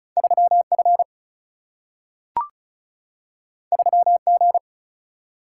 Morse code, heard as a keyed tone, about 36 wpm, 700 Hz. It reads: 3F 3G
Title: Single-Letter Single-Number 35wpm